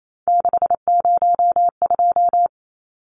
Morse code, heard as a keyed tone, about 28 wpm, 700 Hz. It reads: 602
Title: Numbers - 3-Digit 28wpm